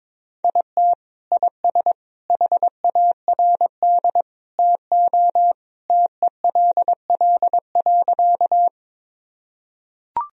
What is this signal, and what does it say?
Morse code, heard as a keyed tone, about 22 wpm, 700 Hz. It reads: IT IS HARD TO TELL.
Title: Sentences from Top 1000 Words - Review Part 1 - 22wpm